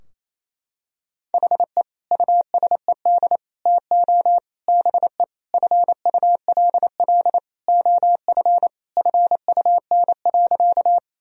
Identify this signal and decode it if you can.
Morse code, heard as a keyed tone, about 28 wpm, 700 Hz. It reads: HE USED TO BE FULL OF FUN.
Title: Sentences from Top 1000 Words - Review Part 4 - 28wpm